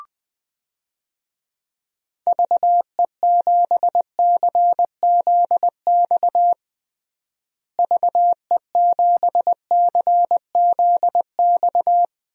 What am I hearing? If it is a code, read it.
Morse code, heard as a keyed tone, about 20 wpm, 700 Hz. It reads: VE7CZX VE7CZX
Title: DXCC-100 Call Signs - 6 Characters - Repeated 2x - 20wpm